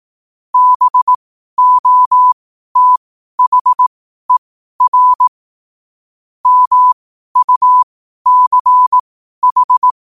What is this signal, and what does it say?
Morse code, heard as a keyed tone, about 18 wpm, spaced as for 12 wpm, 1000 Hz. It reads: BOTHER MUCH
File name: qotdfiles / 12wpm